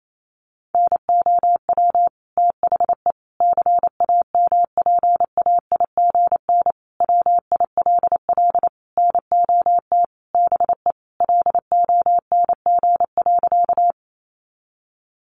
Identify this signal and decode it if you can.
Morse code, heard as a keyed tone, about 28 wpm, 700 Hz. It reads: NOW THE CAMPAIGN WILL NOT BE LONG.
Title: Sentences from Top 1200 Words 28wpm